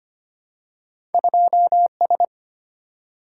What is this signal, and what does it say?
Morse code, heard as a keyed tone, about 25 wpm, 700 Hz. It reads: 2S
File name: Two Letter-Number Combinations - 25wpm